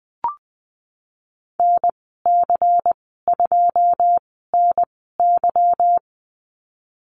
Morse code, heard as a keyed tone, about 20 wpm, spaced as for 15 wpm, 700 Hz. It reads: NC2NY
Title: 173 Lesson 28g - Number 9 - Call Signs 20wpm